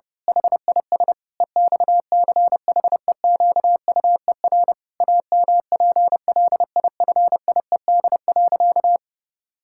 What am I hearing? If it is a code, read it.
Morse code, heard as a keyed tone, about 30 wpm, 700 Hz. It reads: HIS EXCHEQUER AMPLIFIED.
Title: Pangram Challenge v2 - 30wpm